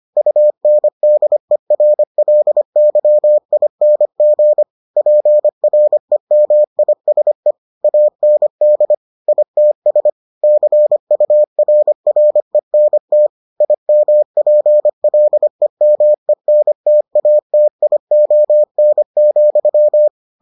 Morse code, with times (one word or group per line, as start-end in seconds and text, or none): 0.2-4.6 UNDERLYING
5.0-7.5 PREMISE
7.8-8.9 AND
9.3-10.1 ITS
10.4-13.3 CURRENT
13.6-20.1 IMPLEMENTATION,